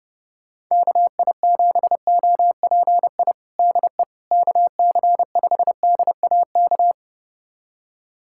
Morse code, heard as a keyed tone, about 32 wpm, 700 Hz. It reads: KI7OPI DE KC5DAK
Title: USA Callsign DE Callsign 30wpm